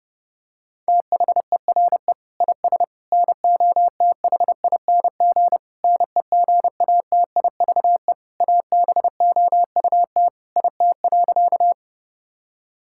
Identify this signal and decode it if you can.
Morse code, heard as a keyed tone, about 30 wpm, 700 Hz. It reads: THERE IS NOTHING NEGATIVE ABOUT IT.
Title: Sentences from Top 1000 Words - Review Part 6 - 30wpm